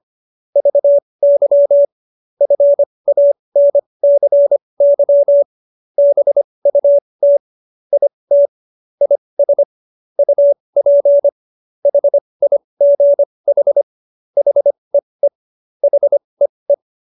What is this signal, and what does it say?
Morse code, heard as a keyed tone, about 25 wpm, spaced as for 20 wpm, 570 Hz. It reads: VY FANCY BUT IT IS UP HIGH HEE HEE